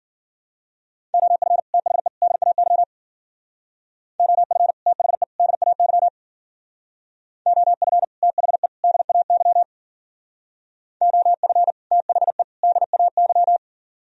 Morse code, heard as a keyed tone, about 53 wpm, 700 Hz. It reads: OF THE DAY OF THE DAY OF THE DAY OF THE DAY